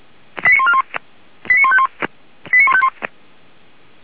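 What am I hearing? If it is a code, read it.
Five-tone selective call (ZVEI), heard as quick runs of five tones, tones of about 70 ms. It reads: 89141 89151 89151
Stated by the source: ZVEI five-tone selective call received on SDR, from feuerwehr-orlamuende.org, a fire brigade.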